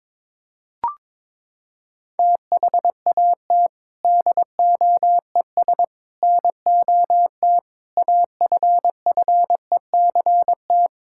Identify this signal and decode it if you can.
Morse code, heard as a keyed tone, about 22 wpm, 700 Hz. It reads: THAT DOES NOT AFFECT
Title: Sentences from Top 1000 Words - Review Part 6 - 22wpm